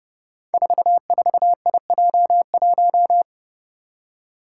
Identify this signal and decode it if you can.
Morse code, heard as a keyed tone, about 30 wpm, 700 Hz. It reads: 44IJ1
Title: Five Challenging Letter-Number Combinations 30wpm